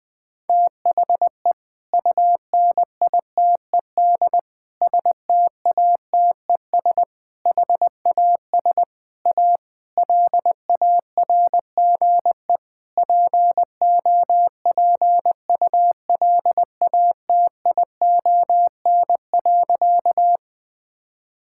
Morse code, heard as a keyed tone, about 20 wpm, 700 Hz. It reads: THE UNITED STATES HAS A LARGE POPULATION.
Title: Sentences with 4-Word Phrases - 20wpm